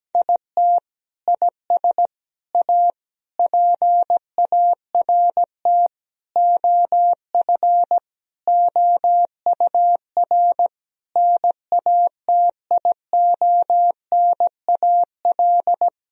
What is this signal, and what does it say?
Morse code, heard as a keyed tone, about 17 wpm, 700 Hz. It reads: IT IS A PART OF OUR NATIONAL
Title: Sentences from Top 1000 Words - Review Part 4 - 17wpm